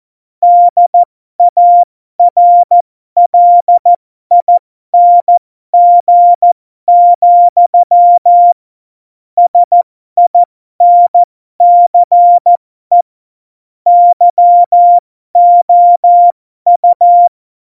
Morse code, heard as a keyed tone, about 14 wpm, spaced as for 12 wpm, 700 Hz. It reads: DARLING, SINCE YOU
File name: Morsekurs / Witze